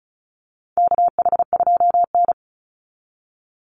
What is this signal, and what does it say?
Morse code, heard as a keyed tone, about 34 wpm, 700 Hz. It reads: KH2N
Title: Call Signs - 35wpm